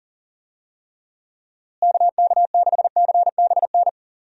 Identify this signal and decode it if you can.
Morse code, heard as a keyed tone, about 40 wpm, 700 Hz. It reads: KK6CBN
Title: USA Call Signs - 6 Characters - Repeated 2x - 40wpm